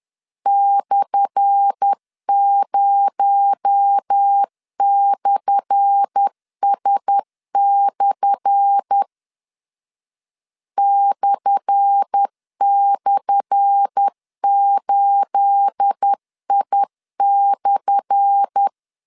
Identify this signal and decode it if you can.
Morse code, heard as a keyed tone, about 10 wpm, 800 Hz. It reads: /0/S/ //8I/
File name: LEZIONI / 10 SIGNS / 10A SLASH 10wpm 2808